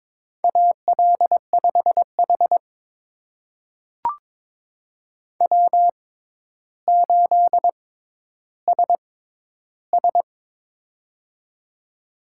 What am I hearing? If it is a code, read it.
Morse code, heard as a keyed tone, about 22 wpm, spaced as for 11 wpm, 700 Hz. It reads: AL5H W8SS
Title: USA Call Signs - 4 Characters - 6x Character Spacing - 22wpm